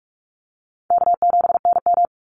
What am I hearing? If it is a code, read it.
Morse code, heard as a keyed tone, about 44 wpm, 700 Hz. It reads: K7NM
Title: Easy Call Signs Repeated 3x - 45wpm